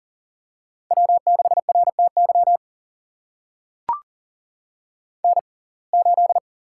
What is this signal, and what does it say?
Morse code, heard as a keyed tone, about 40 wpm, spaced as for 29 wpm, 700 Hz. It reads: W6RTY N8
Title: USA Call Signs - 5 Characters - 6x Character Spacing - 40wpm